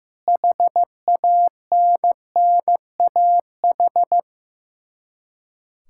Morse code, heard as a keyed tone, about 15 wpm, 700 Hz. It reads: HANNAH
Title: Practice QSO - Exchange 1 - Beginning - 15wpm